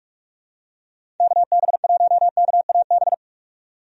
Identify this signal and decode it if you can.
Morse code, heard as a keyed tone, about 46 wpm, 700 Hz. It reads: KB1KAB
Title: USA Call Signs - 6 Characters - 45wpm